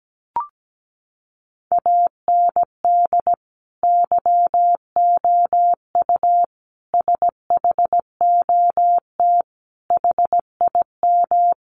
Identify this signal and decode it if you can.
Morse code, heard as a keyed tone, about 17 wpm, 700 Hz. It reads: AND YOU SHOT HIM
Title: Sentences from Top 1500 Words 17wpm